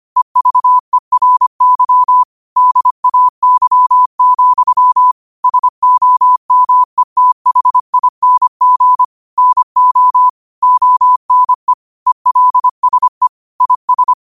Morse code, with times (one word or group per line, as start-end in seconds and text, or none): 0.2-2.2 EVERY
2.6-5.1 DAY,
5.4-9.0 SOMETHING
9.4-10.3 NO
10.6-11.7 ONE
12.1-13.3 ELSE
13.6-14.1 IS